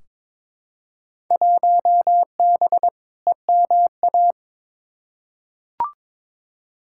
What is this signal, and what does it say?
Morse code, heard as a keyed tone, about 21 wpm, 700 Hz. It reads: 1B EMA
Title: Field Day Exchanges - 22wpm